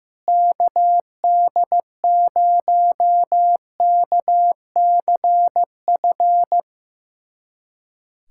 Morse code, heard as a keyed tone, about 15 wpm, 700 Hz. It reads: KD0KCF
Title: USA Callsign DE Callsign - 6 Characters - 15wpm